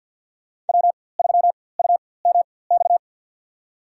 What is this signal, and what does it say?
Morse code, heard as a keyed tone, about 53 wpm, spaced as for 24 wpm, 700 Hz. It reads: W3UMX
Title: USA Call Signs - 5 Characters - 4x Character Spacing - 50wpm